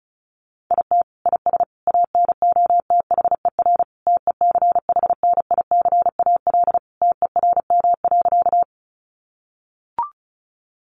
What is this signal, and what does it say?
Morse code, heard as a keyed tone, about 34 wpm, 700 Hz. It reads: IT IS ANOTHER TECHNICAL TERM.
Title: Sentences from Top 1100 Words 35wpm